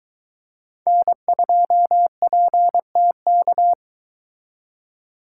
Morse code, heard as a keyed tone, about 22 wpm, 700 Hz. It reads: N2PTK